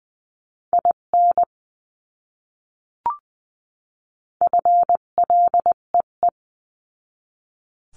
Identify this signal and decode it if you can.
Morse code, heard as a keyed tone, about 20 wpm, 700 Hz. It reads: IN FLEE
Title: 258 Lesson 42c - Words 16-20wpm